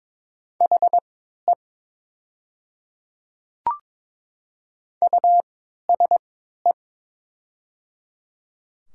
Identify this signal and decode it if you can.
Morse code, heard as a keyed tone, about 22 wpm, spaced as for 12 wpm, 700 Hz. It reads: HE USE
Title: 3x Character Spacing - Top 100 Words - 22wpm